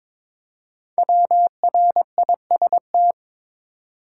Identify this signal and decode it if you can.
Morse code, heard as a keyed tone, about 22 wpm, 700 Hz. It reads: WRIST
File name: 5 Letter Words - First 3 Letters Given - 22wpm